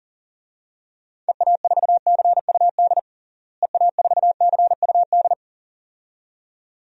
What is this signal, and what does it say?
Morse code, heard as a keyed tone, about 40 wpm, 700 Hz. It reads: EA4CUD EA4CUD
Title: DXCC-100 Call Signs - 6 Characters - Repeated 2x - 40wpm